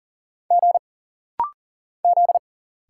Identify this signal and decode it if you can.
Morse code, heard as a keyed tone, about 40 wpm, 700 Hz. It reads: G Z
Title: Single Letter-Number Alt - Rapid-Fire 40wpm